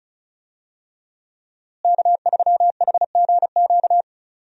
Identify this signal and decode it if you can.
Morse code, heard as a keyed tone, about 36 wpm, 700 Hz. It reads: K3HGQ
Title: USA Call Signs - 5 Characters - Repeated 3x - 35wpm